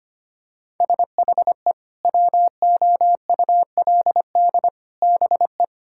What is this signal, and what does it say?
Morse code, heard as a keyed tone, about 25 wpm, 700 Hz. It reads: SHE WOULD BE
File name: practice / 25wpm / Sentences with 4-Word Phrases - 25wpm